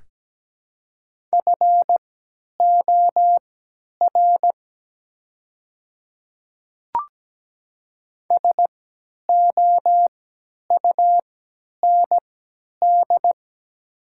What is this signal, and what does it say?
Morse code, heard as a keyed tone, about 17 wpm, spaced as for 10 wpm, 700 Hz. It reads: FOR SOUND
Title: 3x Character Spacing - Top 300 Words - 17wpm